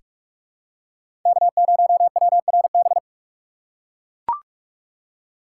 Morse code, heard as a keyed tone, about 43 wpm, 700 Hz. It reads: K0WRB
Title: USA Call Signs - 2x Character Spacing - 45wpm